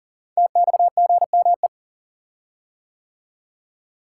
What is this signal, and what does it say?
Morse code, heard as a keyed tone, about 40 wpm, 700 Hz. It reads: TXGME
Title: Code Groups - 5 Letter:Numbers - 40wpm